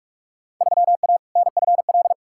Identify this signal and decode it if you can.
Morse code, heard as a keyed tone, about 46 wpm, 700 Hz. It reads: 3A NFL
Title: Field Day Exchanges - 45wpm